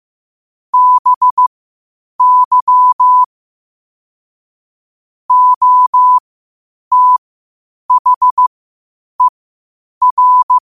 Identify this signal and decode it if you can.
Morse code, heard as a keyed tone, about 15 wpm, spaced as for 8 wpm, 1000 Hz. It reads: BY OTHER
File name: qotdfiles / 8wpm